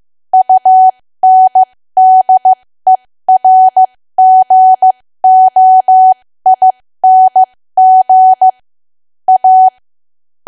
Morse code, heard as a keyed tone, about 15 wpm, spaced as for 13 wpm, 750 Hz. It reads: UNDERGOING A
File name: cwa / W1AW-Level2 / 13WPM